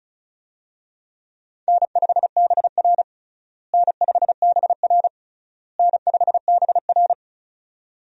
Morse code, heard as a keyed tone, about 36 wpm, 700 Hz. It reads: N5BR N5BR N5BR
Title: USA Call Signs - 4 Characters - Repeated 3x - 35wpm